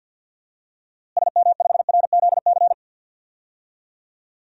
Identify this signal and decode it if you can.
Morse code, heard as a keyed tone, about 52 wpm, 700 Hz. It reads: SM5RZC
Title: DXCC-100 Call Signs - 6 Characters - Repeated 2x - 50wpm